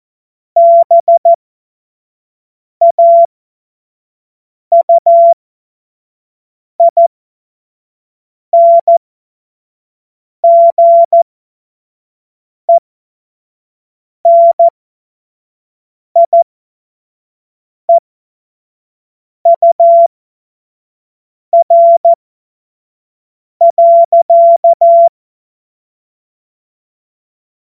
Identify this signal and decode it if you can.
Morse code, heard as a keyed tone, about 14 wpm, spaced as for 5 wpm, 680 Hz. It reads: BAUINGENIEUR.
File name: Morsekurs / Witze / Witz 01. August 2023 25 F70